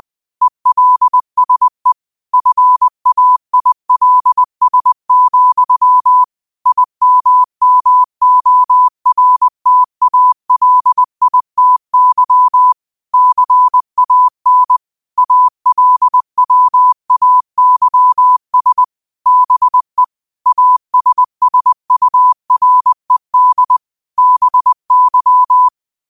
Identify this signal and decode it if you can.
Morse code, heard as a keyed tone, about 20 wpm, 1000 Hz. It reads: ELSE FAILS, IMMORTALITY CAN ALWAYS BE ASSURED BY
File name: qotdfiles / 20wpm